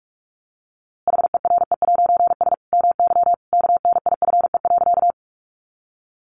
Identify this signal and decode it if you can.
Morse code, heard as a keyed tone, about 46 wpm, 700 Hz. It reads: HERE'S MY KNIFE.
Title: A Sound Of Thunder 45wpm